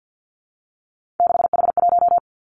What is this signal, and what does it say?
Morse code, heard as a keyed tone, about 50 wpm, 700 Hz. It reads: 6H1
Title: Three Challenging Letter-Number Combinations 50wpm